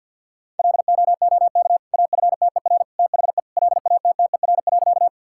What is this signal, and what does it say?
Morse code, heard as a keyed tone, about 51 wpm, 700 Hz. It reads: LOOK AFTER THE LATTER.
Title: Sentences from Top 1600 Words 50wpm